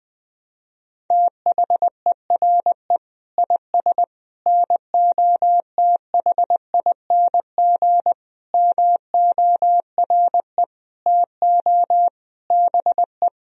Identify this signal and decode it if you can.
Morse code, heard as a keyed tone, about 21 wpm, 700 Hz. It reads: THERE IS NOTHING MORE TO BE
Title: Sentences from Top 500 Words and Abbreviations 20wpm